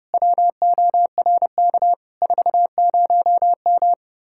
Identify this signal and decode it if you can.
Morse code, heard as a keyed tone, about 37 wpm, 700 Hz. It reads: WORK 40M
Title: Speed-Racing Common QSO Phrases - 30wpm